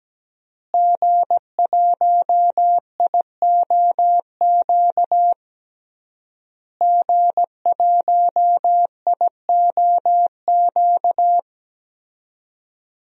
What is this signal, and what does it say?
Morse code, heard as a keyed tone, about 17 wpm, 700 Hz. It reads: G1IOQ G1IOQ
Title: DXCC-100 Call Signs - 5 Characters - Repeated 3x - 17wpm